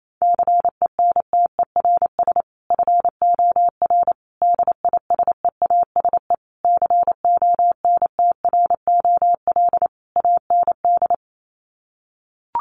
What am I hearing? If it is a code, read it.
Morse code, heard as a keyed tone, about 28 wpm, 700 Hz. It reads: CENTERS FOR DISEASE CONTROL AND
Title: Sets of 5 Words Encore2 28wpm